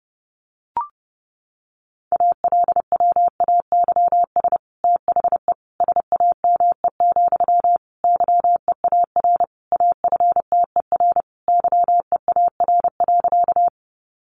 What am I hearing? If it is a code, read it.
Morse code, heard as a keyed tone, about 28 wpm, 700 Hz. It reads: ALWAYS THE SAME, YEAR AFTER YEAR.
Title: Sentences from Top 300 Words 30wpm